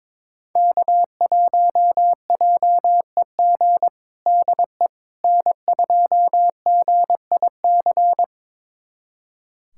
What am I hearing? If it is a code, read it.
Morse code, heard as a keyed tone, about 22 wpm, 700 Hz. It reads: K1JEG DE N2GIC
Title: USA Callsign DE Callsign 22wpm